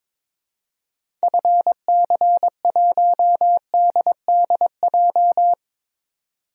Morse code, heard as a keyed tone, about 22 wpm, 700 Hz. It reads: FC1DDJ
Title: DXCC-100 Call Signs - 6 Characters - 22wpm